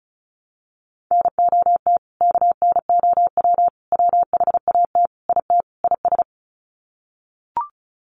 Morse code, heard as a keyed tone, about 34 wpm, 700 Hz. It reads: NOT KNOW WHAT IT IS
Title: Sets of 5 Words Encore 35wpm